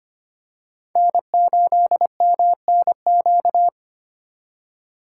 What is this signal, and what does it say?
Morse code, heard as a keyed tone, about 25 wpm, 700 Hz. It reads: N8MNQ
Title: USA Call Signs - 5 Characters - 2x Character Spacing - 25wpm